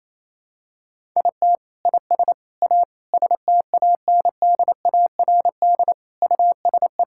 Morse code, heard as a keyed tone, about 28 wpm, 700 Hz. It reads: IT IS A STANDARD USE
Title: Sentences from Top 1000 Words - Review Part 6 - 28wpm